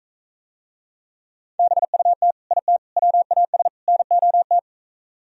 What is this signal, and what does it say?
Morse code, heard as a keyed tone, about 45 wpm, 700 Hz. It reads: BUT IT WAS NOT